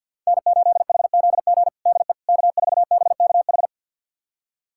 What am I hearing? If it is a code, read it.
Morse code, heard as a keyed tone, about 52 wpm, 700 Hz. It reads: N8HZG DE K4BKH
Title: USA Callsign DE Callsign 50wpm